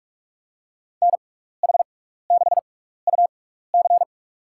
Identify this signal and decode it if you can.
Morse code, heard as a keyed tone, about 41 wpm, spaced as for 15 wpm, 700 Hz. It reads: NH6UC
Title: USA Call Signs - 6x Character Spacing - 45wpm